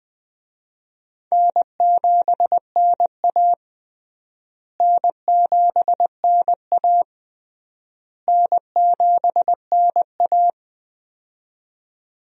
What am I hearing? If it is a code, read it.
Morse code, heard as a keyed tone, about 20 wpm, 700 Hz. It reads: N7NA N7NA N7NA
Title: USA Call Signs - 4 Characters - Repeated 3x - 20wpm